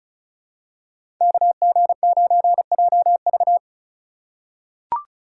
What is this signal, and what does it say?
Morse code, heard as a keyed tone, about 36 wpm, 700 Hz. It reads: KG9JV
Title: USA Call Signs - 2x Character Spacing - 35wpm